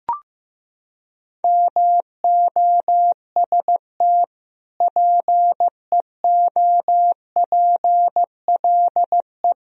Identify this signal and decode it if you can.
Morse code, heard as a keyed tone, about 16 wpm, 700 Hz. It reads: MOST PEOPLE
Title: Sentences from Top 1000 Words - Review Part 1 - 15wpm